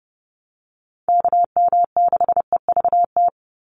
Morse code, heard as a keyed tone, about 30 wpm, 700 Hz. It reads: KM6EVT
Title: USA Call Signs 30wpm